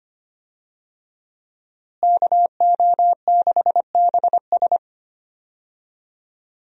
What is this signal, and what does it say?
Morse code, heard as a keyed tone, about 25 wpm, 700 Hz. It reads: KO6BS
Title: USA Call Signs - 5 Characters - Repeated 3x - 25wpm